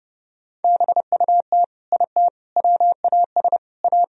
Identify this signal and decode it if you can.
Morse code, heard as a keyed tone, about 30 wpm, 700 Hz. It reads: BUT IT WAS A
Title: Sentences from Top 1000 Words - Review Part 4 - 30wpm